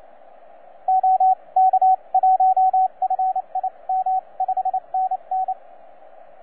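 Morse code, heard as a keyed tone, about 28 wpm, spaced as for 24 wpm, 750 Hz, with radio noise. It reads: OK1FIM 5NN
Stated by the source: TZ6BB on 12 metres CW, from DX World.net.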